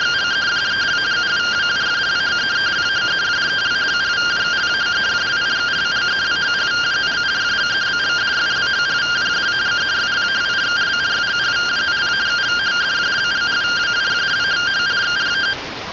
Two-tone teleprinter signal, about 170 Hz shift.
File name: findings / 202 coronal iodine 3mm.gen-42 rtty rx